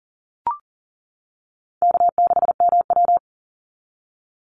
Morse code, heard as a keyed tone, about 40 wpm, 700 Hz. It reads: K6MW
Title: NA SOTA Chaser Callsigns 40wpm